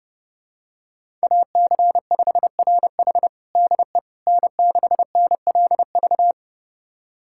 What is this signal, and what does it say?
Morse code, heard as a keyed tone, about 32 wpm, 700 Hz. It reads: AC5RH DE N6NLV
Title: USA Callsign DE Callsign 30wpm